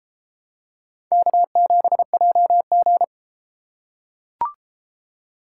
Morse code, heard as a keyed tone, about 33 wpm, 700 Hz. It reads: K7JG